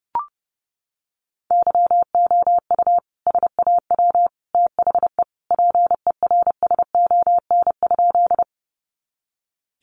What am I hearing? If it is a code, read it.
Morse code, heard as a keyed tone, about 30 wpm, 700 Hz. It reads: YOU SAW THE PERSON?
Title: Sentences from Top 400 Words 30wpm